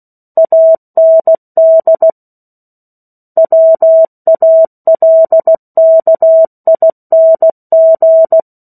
Morse code, heard as a keyed tone, about 16 wpm, 650 Hz. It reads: AND WALKING
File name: 03-Chapter-03 16 WPM